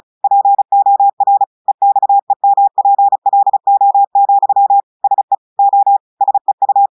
Morse code, heard as a keyed tone, about 36 wpm, 800 Hz. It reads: POR EXEMPLO, SE O SEU
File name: portugais texte en clair 2 vitesse 35 mots minutes